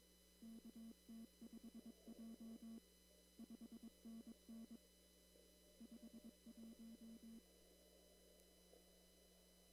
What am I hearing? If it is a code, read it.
Morse code, heard as a keyed tone, about 22 wpm, 230 Hz. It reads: KT5J 5NN 51
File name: CW / kt5j wpx cw-op-k5pi